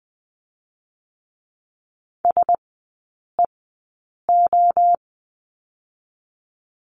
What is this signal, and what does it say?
Morse code, heard as a keyed tone, about 20 wpm, spaced as for 8 wpm, 700 Hz. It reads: SEO
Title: 041 Lesson 8e - Number 1 - Three Characters 20wpm